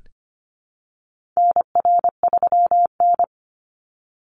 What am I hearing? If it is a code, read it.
Morse code, heard as a keyed tone, about 25 wpm, 700 Hz. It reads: NR3N
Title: Call Signs Repeated 3x - 25wpm